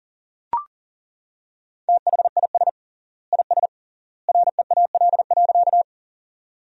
Morse code, heard as a keyed tone, about 40 wpm, 700 Hz. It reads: THIS IS REAL.
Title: 3x Word-Spacing - Sentences from Top 300 Words - 40wpm